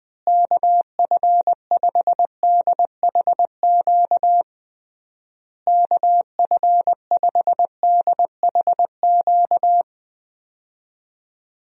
Morse code, heard as a keyed tone, about 20 wpm, 700 Hz. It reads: KF5DHQ KF5DHQ
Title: USA Call Signs - 6 Characters - Repeated 3x - 20wpm